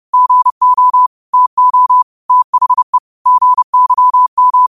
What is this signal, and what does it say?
Morse code, heard as a keyed tone, about 30 wpm, 1000 Hz. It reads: GO TO THE GYM
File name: qotdfiles / 30wpm